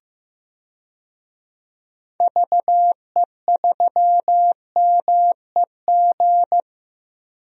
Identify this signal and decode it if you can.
Morse code, heard as a keyed tone, about 15 wpm, 700 Hz. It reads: VE3MEG
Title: DXCC-100 Call Signs - 6 Characters - 15wpm